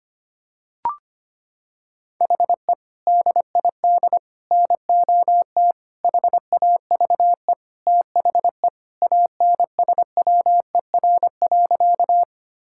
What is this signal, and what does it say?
Morse code, heard as a keyed tone, about 25 wpm, 700 Hz. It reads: HE DID NOT HAVE THE ANSWER.
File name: practice / 25wpm / Sentences with 4-Word Phrases - 25wpm